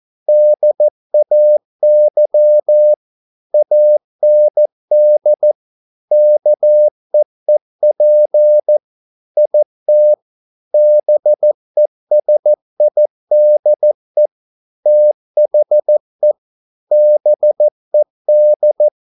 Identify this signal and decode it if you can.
Morse code, heard as a keyed tone, about 14 wpm, 600 Hz. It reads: DAY AND KEEP IT BESIDE THE BED